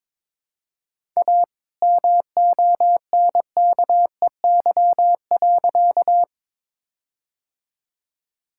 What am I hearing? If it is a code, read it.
Morse code, heard as a keyed tone, about 22 wpm, 700 Hz. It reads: A MONKEY.
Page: morse-code-ninja-practice-22wpm